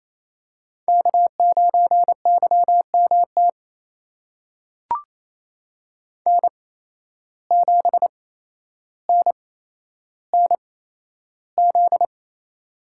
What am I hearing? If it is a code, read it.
Morse code, heard as a keyed tone, about 28 wpm, spaced as for 12 wpm, 700 Hz. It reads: K9YMT N7NNZ
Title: USA Call Signs - 5 Characters - 8x Character Spacing - 28wpm